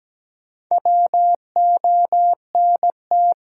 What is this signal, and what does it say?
Morse code, heard as a keyed tone, about 17 wpm, 700 Hz. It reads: WONT
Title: Sentences from Top 1000 Words - Review Part 6 - New Word Drill - 17wpm